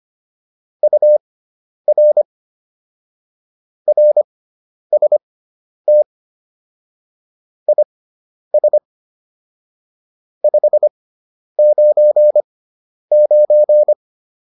Morse code, heard as a keyed tone, about 25 wpm, spaced as for 10 wpm, 600 Hz. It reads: UR RST IS 599
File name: QSO108 10